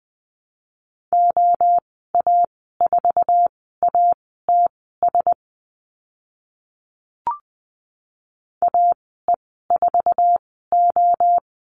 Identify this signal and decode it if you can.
Morse code, heard as a keyed tone, about 20 wpm, spaced as for 14 wpm, 700 Hz. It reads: OA4ATS AE4O
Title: 062 Lesson 11g - Letter h - Call Signs 20wpm